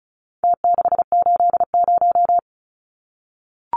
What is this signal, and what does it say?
Morse code, heard as a keyed tone, about 34 wpm, 700 Hz. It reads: T680
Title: Numbers - Contest Style - 4-Digit 35wpm